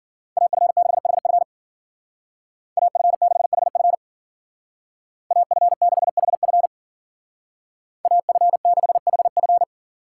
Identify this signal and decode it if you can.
Morse code, heard as a keyed tone, about 53 wpm, 700 Hz. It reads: AF6HF AF6HF AF6HF AF6HF